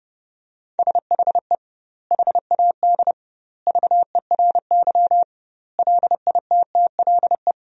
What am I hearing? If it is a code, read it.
Morse code, heard as a keyed tone, about 30 wpm, 700 Hz. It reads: SHE HAD VERY LITTLE
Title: 2x Word-Spacing - Sentences from Top 400 Words - 30wpm